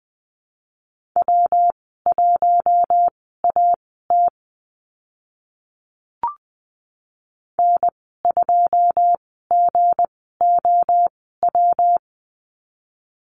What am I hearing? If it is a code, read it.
Morse code, heard as a keyed tone, about 20 wpm, spaced as for 15 wpm, 700 Hz. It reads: W1AT N2GOW
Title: 167 Lesson 27g - Number 7 - Call Signs 20wpm